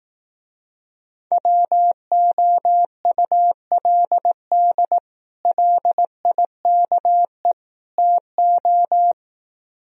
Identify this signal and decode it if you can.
Morse code, heard as a keyed tone, about 18 wpm, 700 Hz. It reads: WOULD LIKE TO
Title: Speed-Racing - Sets of 3 Words 15wpm